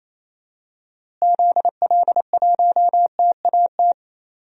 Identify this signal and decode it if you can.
Morse code, heard as a keyed tone, about 28 wpm, 700 Hz. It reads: ZL1TAT
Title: DXCC-100 Call Signs - 6 Characters - Repeated 3x - 28wpm